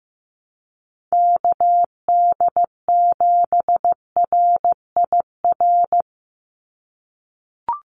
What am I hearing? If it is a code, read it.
Morse code, heard as a keyed tone, about 15 wpm, 700 Hz. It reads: KD7RIR
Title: USA Call Signs 15wpm